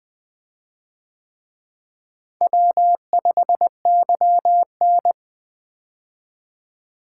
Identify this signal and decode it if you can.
Morse code, heard as a keyed tone, about 24 wpm, 700 Hz. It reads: W5YN
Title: Speed-Racing - Four Letter-Number Combinations - 20wpm